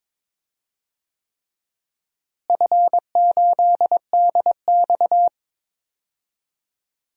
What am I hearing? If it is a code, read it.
Morse code, heard as a keyed tone, about 22 wpm, 700 Hz. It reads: F8DX
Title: DXCC-100 Call Signs - 4 Characters - Repeated 3x - 22wpm